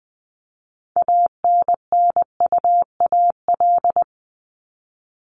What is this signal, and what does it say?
Morse code, heard as a keyed tone, about 20 wpm, 700 Hz. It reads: ANNUAL
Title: Top 1251-1500 Words 20wpm